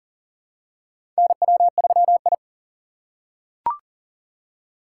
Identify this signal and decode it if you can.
Morse code, heard as a keyed tone, about 40 wpm, 700 Hz. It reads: NW3I
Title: USA Call Signs - 4 Characters - 3x Character Spacing - 40wpm